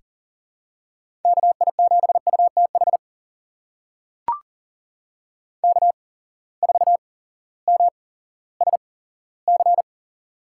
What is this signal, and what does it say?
Morse code, heard as a keyed tone, about 40 wpm, spaced as for 18 wpm, 700 Hz. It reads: KI7UTH K4MSC
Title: USA Call Signs - 8x Character Spacing - 40wpm